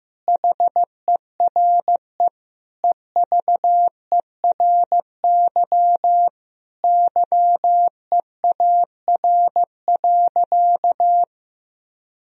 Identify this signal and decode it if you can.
Morse code, heard as a keyed tone, about 15 wpm, 700 Hz. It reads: HERE EVERY YEAR.
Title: Sentences from Top 1000 Words - Review Part 1 - 15wpm